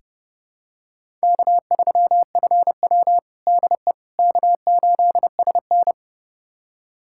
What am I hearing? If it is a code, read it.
Morse code, heard as a keyed tone, about 32 wpm, 700 Hz. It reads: K3FW DE K8SN
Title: USA Callsign DE Callsign - 4 Characters - 30wpm